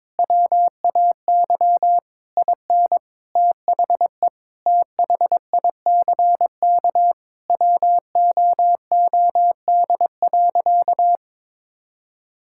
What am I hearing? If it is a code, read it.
Morse code, heard as a keyed tone, about 22 wpm, 700 Hz. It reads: WAY IN THE THICK WOOD.
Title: Sentences from Top 1600 Words 22wpm